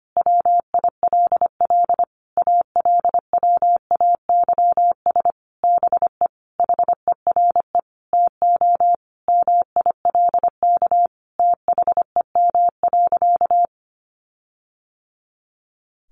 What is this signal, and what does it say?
Morse code, heard as a keyed tone, about 25 wpm, 700 Hz. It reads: WILL ALWAYS BE HERE TO MILK THEM.
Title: Sentences from Top 1500 Words 25wpm